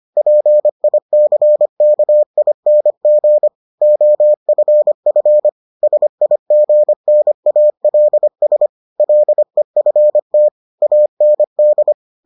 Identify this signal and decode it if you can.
Morse code, heard as a keyed tone, about 25 wpm, 600 Hz. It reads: PICKING OFF SIGNALS LEFT AND